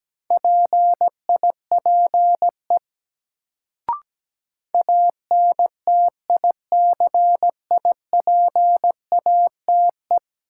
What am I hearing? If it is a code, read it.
Morse code, heard as a keyed tone, about 17 wpm, 700 Hz. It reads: PIPE ANTICIPATE
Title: Sentences from Top 1500 Words - New Word Drill Alt - 17wpm